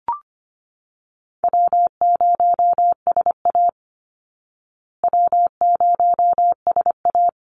Morse code, heard as a keyed tone, about 25 wpm, 700 Hz. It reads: W0SA W0SA
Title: Easy Call Signs Repeated 2x - 25wpm